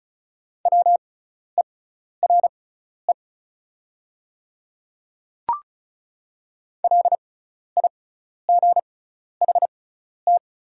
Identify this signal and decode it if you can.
Morse code, heard as a keyed tone, about 36 wpm, spaced as for 12 wpm, 690 Hz. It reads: WERE LIGHT
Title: 6x Character Spacing - Top 200 Words - 35wpm